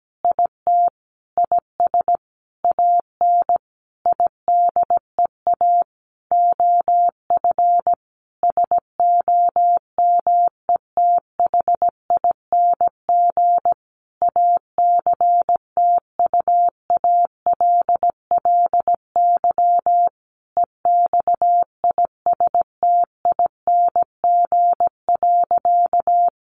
Morse code, heard as a keyed tone, about 17 wpm, 700 Hz. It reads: IT IS AN IDEA OF SOMETHING ACTUALLY EXISTING.
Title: Sentences from Top 1500 Words 17wpm